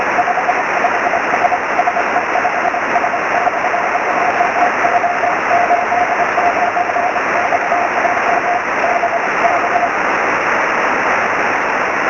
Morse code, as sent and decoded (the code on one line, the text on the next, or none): none